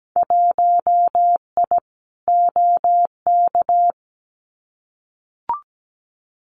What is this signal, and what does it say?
Morse code, heard as a keyed tone, about 17 wpm, 700 Hz. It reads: 1I OK
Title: Winter Field Day Exchanges Encore 17wpm